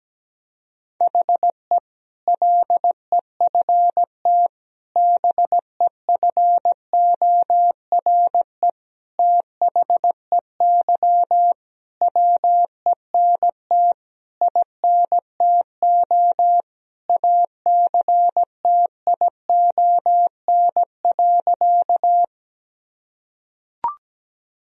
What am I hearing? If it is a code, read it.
Morse code, heard as a keyed tone, about 17 wpm, 700 Hz. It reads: HE LEFT BEFORE THEY WENT INTO ACTION.
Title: Sentences from Top 1000 Words - Review Part 4 - 17wpm